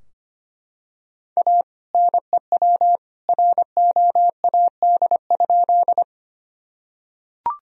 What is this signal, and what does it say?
Morse code, heard as a keyed tone, about 25 wpm, 700 Hz. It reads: A NEW ROAD?
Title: Sentences from Top 1000 Words - Review Part 2 - 25wpm